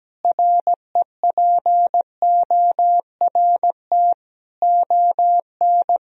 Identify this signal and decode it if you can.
Morse code, heard as a keyed tone, about 17 wpm, 700 Hz. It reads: REPORT ON
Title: Sentences from Top 1000 Words - Review Part 4 - 17wpm